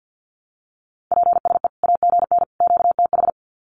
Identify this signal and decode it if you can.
Morse code, heard as a keyed tone, about 50 wpm, 700 Hz. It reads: PSE AGN QTH
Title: Common QSO Phrases 50wpm